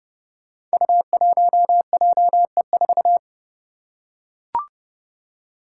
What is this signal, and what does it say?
Morse code, heard as a keyed tone, about 30 wpm, 700 Hz. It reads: U1JE4
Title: Five Challenging Letter-Number Combinations 30wpm